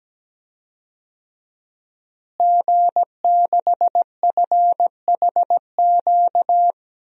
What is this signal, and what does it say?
Morse code, heard as a keyed tone, about 17 wpm, 700 Hz. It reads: G6FHQ
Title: DXCC-100 Call Signs - 5 Characters - Repeated 3x - 17wpm